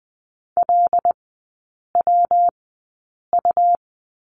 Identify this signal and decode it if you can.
Morse code, heard as a keyed tone, about 20 wpm, spaced as for 8 wpm, 690 Hz. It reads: LWU
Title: 159 Lesson 26e - Letter g - Three Characters 20wpm